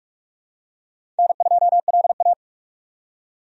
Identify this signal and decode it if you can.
Morse code, heard as a keyed tone, about 42 wpm, 700 Hz. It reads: N2LA
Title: USA Call Signs - 4 Characters - 45wpm